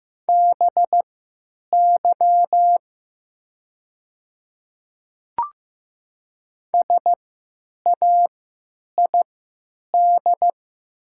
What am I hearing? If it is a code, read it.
Morse code, heard as a keyed tone, about 15 wpm, spaced as for 9 wpm, 700 Hz. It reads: BY SAID
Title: 3x Character Spacing - Top 100 Words - 15wpm